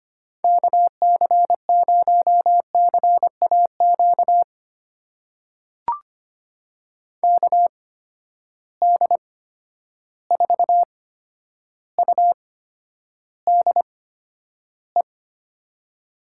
Morse code, heard as a keyed tone, about 25 wpm, spaced as for 11 wpm, 700 Hz. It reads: KC0CAQ KD4UDE
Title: USA Call Signs - 6 Characters - 8x Character Spacing - 25wpm